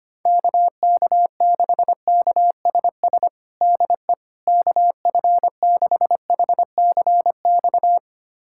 Morse code, heard as a keyed tone, about 25 wpm, 700 Hz. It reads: KK6KSS DE KF6HCX
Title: USA Callsign DE Callsign - 6 Characters - 25wpm